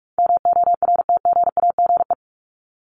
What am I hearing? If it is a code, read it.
Morse code, heard as a keyed tone, about 44 wpm, 700 Hz. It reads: MORTGAGE
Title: Top 1751-2000 Words 45wpm